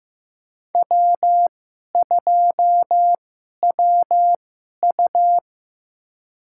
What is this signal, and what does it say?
Morse code, heard as a keyed tone, about 15 wpm, spaced as for 11 wpm, 700 Hz. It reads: W2WU
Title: USA Call Signs - 2x Character Spacing - 15wpm